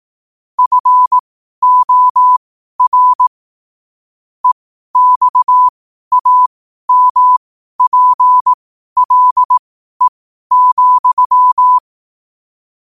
Morse code, heard as a keyed tone, about 18 wpm, spaced as for 12 wpm, 1000 Hz. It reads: FOR EXAMPLE,
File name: qotdfiles / 12wpm